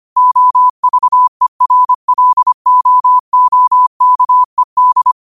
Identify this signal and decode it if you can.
Morse code, heard as a keyed tone, about 25 wpm, 1000 Hz. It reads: OVERLOOKED